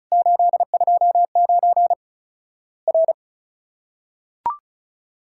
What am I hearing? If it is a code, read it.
Morse code, heard as a keyed tone, about 36 wpm, 700 Hz. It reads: 829 R
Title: Worked All Europe DX Contest - QTC Exchange - 35wpm